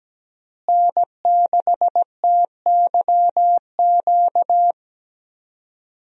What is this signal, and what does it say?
Morse code, heard as a keyed tone, about 17 wpm, 700 Hz. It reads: N6TYQ
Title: USA Call Signs - 5 Characters - Repeated 2x - 17wpm